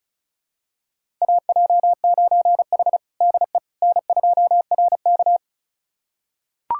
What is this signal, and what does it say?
Morse code, heard as a keyed tone, about 36 wpm, 700 Hz. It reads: AJ9H DE N2RK
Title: USA Callsign DE Callsign - 4 Characters - 35wpm